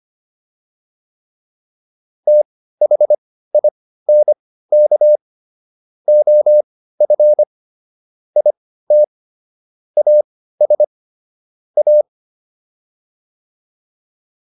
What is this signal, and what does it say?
Morse code, heard as a keyed tone, about 25 wpm, spaced as for 15 wpm, 600 Hz. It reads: THINK OF IT AS A
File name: WD404 15